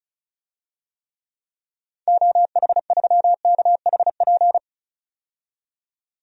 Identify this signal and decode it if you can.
Morse code, heard as a keyed tone, about 34 wpm, 700 Hz. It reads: OH3KHP
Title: DXCC-100 Call Signs - 6 Characters - 35wpm